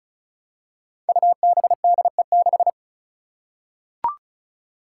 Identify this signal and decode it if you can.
Morse code, heard as a keyed tone, about 36 wpm, 700 Hz. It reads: UBDE6
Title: Five Challenging Letter-Number Combinations 35wpm